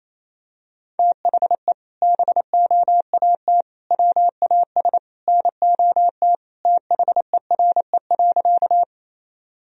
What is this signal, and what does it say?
Morse code, heard as a keyed tone, about 28 wpm, 700 Hz. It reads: THE BOAT WAS NOT THERE.
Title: Sentences from Top 1000 Words - Review Part 3 - 28wpm